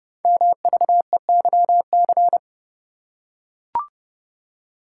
Morse code, heard as a keyed tone, about 30 wpm, 700 Hz. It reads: MVEYC
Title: Code Groups - 5 Letters - 30wpm